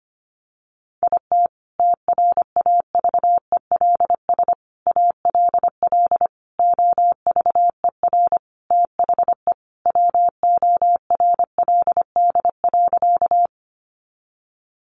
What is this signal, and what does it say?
Morse code, heard as a keyed tone, about 25 wpm, 700 Hz. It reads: IT TRAVELS ALL OVER THE WORLD.
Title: Additional Easy Phrases 25wpm